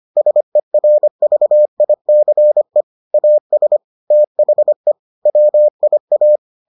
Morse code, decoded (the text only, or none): SERVICE AS THE WIA